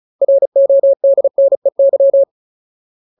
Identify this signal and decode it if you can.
Morse code, heard as a keyed tone, about 34 wpm, 550 Hz. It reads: RODNEY
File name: Ultimate Word List / 35WPM Ultimate